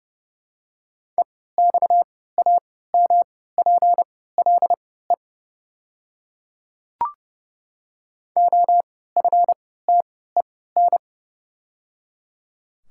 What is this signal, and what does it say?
Morse code, heard as a keyed tone, about 30 wpm, spaced as for 17 wpm, 700 Hz. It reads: EXAMPLE OFTEN
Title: 3x Character Spacing - Top 300 Words - 30wpm